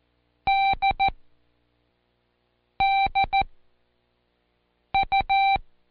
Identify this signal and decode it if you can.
Morse code, heard as a keyed tone, about 14 wpm, spaced as for 5 wpm, 780 Hz. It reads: DDU
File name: Code Quick / AudioDownld1 / Seg23-Day3-Intro D